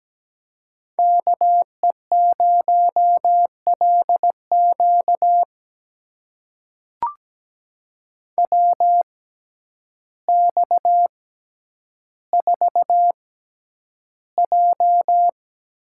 Morse code, heard as a keyed tone, about 17 wpm, spaced as for 9 wpm, 700 Hz. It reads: KE0LQ WX4J
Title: USA Call Signs - 6x Character Spacing - 17wpm